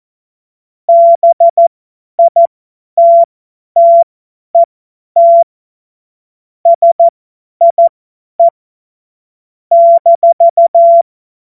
Morse code, heard as a keyed tone, about 14 wpm, spaced as for 10 wpm, 680 Hz. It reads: BITTET SIE -
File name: Morsekurs / Witze / Witz 11. Juli 2023 50 F70